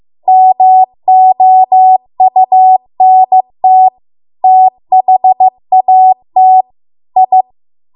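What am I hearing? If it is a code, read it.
Morse code, heard as a keyed tone, about 15 wpm, 750 Hz. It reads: MOUNT THAT I
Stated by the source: Morse code practice at 15 wpm from the national amateur radio association's code archive.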